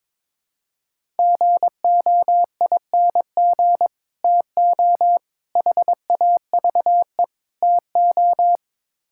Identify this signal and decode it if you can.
Morse code, heard as a keyed tone, about 22 wpm, 700 Hz. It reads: GOING TO HAVE TO
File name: practice / 22wpm / Build A Phrase - 5-Word Phrases - 22wpm